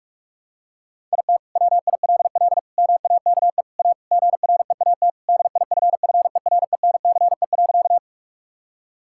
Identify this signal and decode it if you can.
Morse code, heard as a keyed tone, about 45 wpm, 700 Hz. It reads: IT WILL MAKE A GREAT DIFFERENCE.
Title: Sentences from Top 1000 Words - Review Part 5 - 45wpm